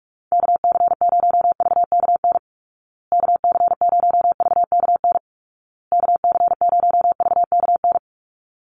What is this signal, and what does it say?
Morse code, heard as a keyed tone, about 44 wpm, 700 Hz. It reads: KC0VKN KC0VKN KC0VKN
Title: Call Signs Repeated 2x - 45wpm